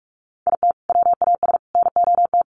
Morse code, heard as a keyed tone, about 44 wpm, 700 Hz. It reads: IT WAS NOT
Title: Sets of 3 Words 45wpm